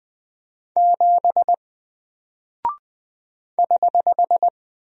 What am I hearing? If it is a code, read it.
Morse code, heard as a keyed tone, about 20 wpm, 700 Hz. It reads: 7 <HH>
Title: Single Character (Letter:Number:Punctuation:Prosign) Alt 20wpm